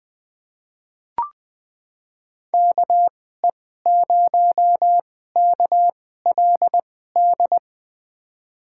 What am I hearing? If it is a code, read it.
Morse code, heard as a keyed tone, about 21 wpm, spaced as for 15 wpm, 700 Hz. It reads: KE0KLD
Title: USA Call Signs - 6 Characters - 2x Character Spacing - 20wpm